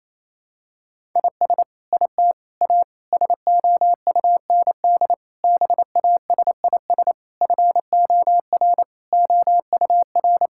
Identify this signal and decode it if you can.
Morse code, heard as a keyed tone, about 27 wpm, 700 Hz. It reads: IS IT A SOUND BASIS FOR OUR
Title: Sentences from Top 1000 Words - Review Part 5 - 28wpm